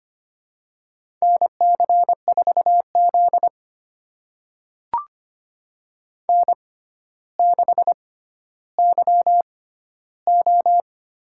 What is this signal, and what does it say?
Morse code, heard as a keyed tone, about 25 wpm, spaced as for 13 wpm, 700 Hz. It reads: NC4Z N6YO
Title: USA Call Signs - 4 Characters - 6x Character Spacing - 25wpm